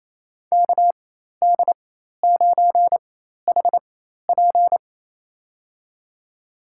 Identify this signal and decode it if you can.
Morse code, heard as a keyed tone, about 28 wpm, spaced as for 13 wpm, 700 Hz. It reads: KD9HP
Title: USA Call Signs - 5 Characters - 4x Character Spacing - 28wpm